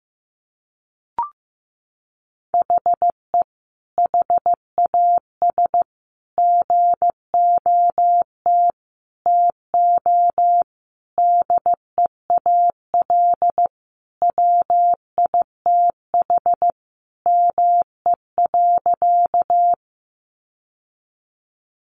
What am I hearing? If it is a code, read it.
Morse code, heard as a keyed tone, about 14 wpm, 700 Hz. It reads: HE HAS GOT TO DEAL WITH ME.
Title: Sentences from Top 800 Words 15wpm